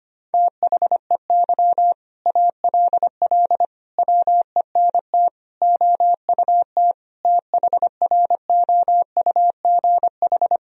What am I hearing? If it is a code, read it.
Morse code, heard as a keyed tone, about 25 wpm, 700 Hz. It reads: THEY ALL WENT OUT THROUGH
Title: Sentences from Top 1000 Words - Review Part 1 - 25wpm